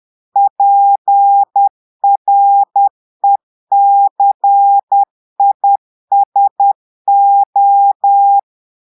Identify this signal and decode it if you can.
Morse code, heard as a keyed tone, about 10 wpm, 800 Hz. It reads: PRECISO
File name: portugais texte en clair 2 vitesse 10 mots minutes